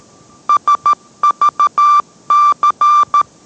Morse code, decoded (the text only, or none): SVC